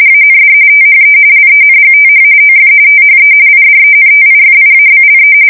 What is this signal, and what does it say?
RTTY, about 45.45 baud. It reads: AND THE LENGTH OF THE IDENTIFIABLE
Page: 170 Shift 2125/2295